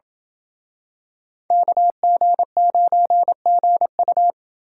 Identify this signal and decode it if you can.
Morse code, heard as a keyed tone, about 28 wpm, 700 Hz. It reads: KG9GU